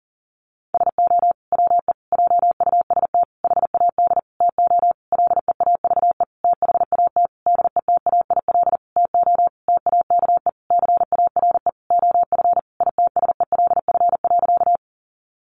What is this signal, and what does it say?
Morse code, heard as a keyed tone, about 40 wpm, 700 Hz. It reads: SO WE JUST HAD TO LEAVE THAT DETAIL TO TAKE CARE OF ITSELF.
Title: Sentences from Top 1500 Words 40wpm